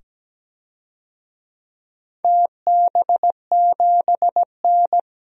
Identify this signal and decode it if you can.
Morse code, heard as a keyed tone, about 17 wpm, 700 Hz. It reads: TB7N
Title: DXCC-100 Call Signs - 4 Characters - 17wpm